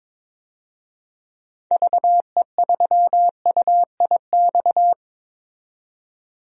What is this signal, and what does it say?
Morse code, heard as a keyed tone, about 22 wpm, 700 Hz. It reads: VE3UIX
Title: DXCC-100 Call Signs - 6 Characters - 22wpm